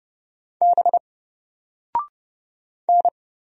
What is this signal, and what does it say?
Morse code, heard as a keyed tone, about 30 wpm, 700 Hz. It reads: B N
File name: Single-Letter-Number-30wpm